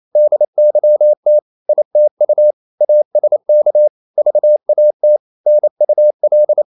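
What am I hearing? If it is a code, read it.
Morse code, heard as a keyed tone, about 28 wpm, 600 Hz. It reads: DYT ITU ASK VAT NUL
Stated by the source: Danish short words sent in Morse at 28 wpm, from CW med Gnister.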